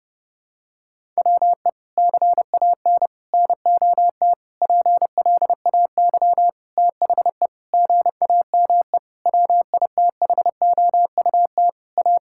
Morse code, heard as a keyed tone, about 30 wpm, 700 Hz. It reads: WE CAN NOT PLAY THE GAME WITHOUT A
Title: Sentences from Top 1000 Words - Review Part 6 - 30wpm